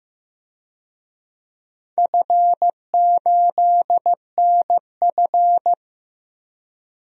Morse code, heard as a keyed tone, about 15 wpm, 700 Hz. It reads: F8NF
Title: DXCC-100 Call Signs - 4 Characters - Repeated 3x - 15wpm